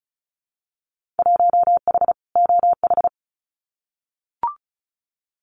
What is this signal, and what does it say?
Morse code, heard as a keyed tone, about 36 wpm, 700 Hz. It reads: 1H OH
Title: Winter Field Day Exchanges 35wpm